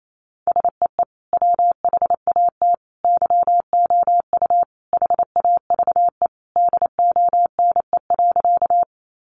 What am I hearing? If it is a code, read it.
Morse code, heard as a keyed tone, about 28 wpm, 700 Hz. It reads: SEE WHAT YOU HAVE DONE.
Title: Sentences from Top 500 Words 28wpm